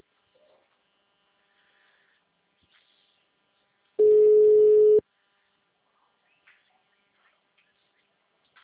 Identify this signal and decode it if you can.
Ringback tone (Europe, 425 Hz).